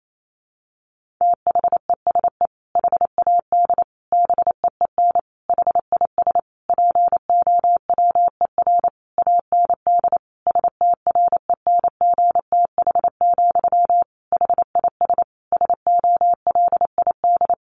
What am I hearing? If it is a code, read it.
Morse code, heard as a keyed tone, about 28 wpm, 700 Hz. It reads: THESE HAD BEEN HIS POWER AND STRENGTH, HIS SOLID
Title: Sentences from Top 1500 Words 28wpm